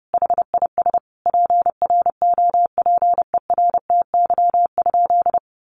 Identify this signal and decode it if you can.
Morse code, heard as a keyed tone, about 30 wpm, 700 Hz. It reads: HIS PROPERTY?
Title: Sentences from Top 1500 Words 30wpm